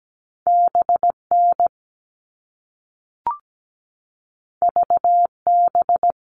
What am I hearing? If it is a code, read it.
Morse code, heard as a keyed tone, about 17 wpm, 700 Hz. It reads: BN VB
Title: All Cut Numbers - 2-Digit 17wpm